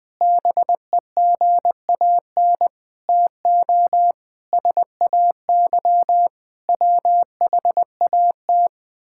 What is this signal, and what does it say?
Morse code, heard as a keyed tone, about 20 wpm, 700 Hz. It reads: BEGAN TO SAY WHAT
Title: Sentences from Top 1000 Words - Review Part 5 - 20wpm